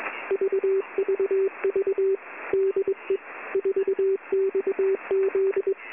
Morse code, heard as a keyed tone, about 21 wpm, 390 Hz. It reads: VVV DE 4XZ
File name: Short Wave Radio Noise 4